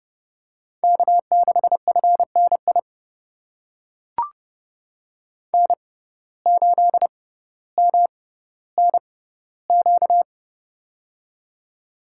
Morse code, heard as a keyed tone, about 32 wpm, spaced as for 16 wpm, 700 Hz. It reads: K6FNI N8MNQ
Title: USA Call Signs - 5 Characters - 6x Character Spacing - 30wpm